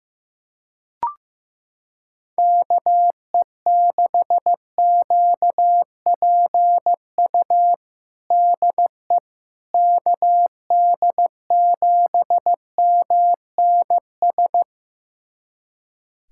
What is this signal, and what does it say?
Morse code, heard as a keyed tone, about 15 wpm, 700 Hz. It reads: KE6QPU DE KD7MNS
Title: USA Callsign DE Callsign 15wpm